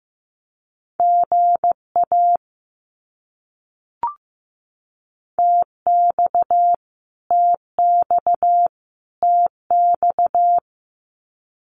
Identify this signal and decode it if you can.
Morse code, heard as a keyed tone, about 15 wpm, 700 Hz. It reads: GA TX TX TX
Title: Abbreviations Repeated 3x - 15wpm